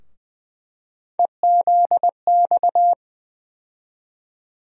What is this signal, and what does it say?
Morse code, heard as a keyed tone, about 20 wpm, 700 Hz. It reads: EZX
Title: USA Call Sign Postfixes - 3 Letters 20wpm